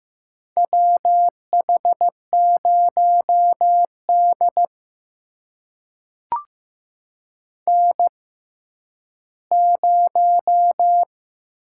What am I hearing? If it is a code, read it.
Morse code, heard as a keyed tone, about 15 wpm, spaced as for 10 wpm, 700 Hz. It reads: WH0D N0
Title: USA Call Signs - 6x Character Spacing - 15wpm